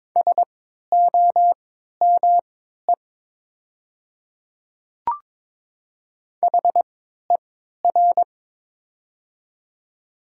Morse code, heard as a keyed tone, about 22 wpm, spaced as for 12 wpm, 700 Hz. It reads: SOME HER
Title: 3x Character Spacing - Top 100 Words - 22wpm